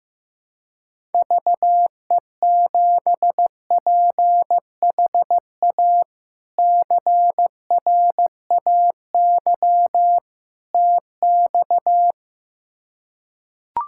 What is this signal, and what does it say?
Morse code, heard as a keyed tone, about 16 wpm, 700 Hz. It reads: VE7PHA CRAY TX
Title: K1USN SST QSO Practice - 15wpm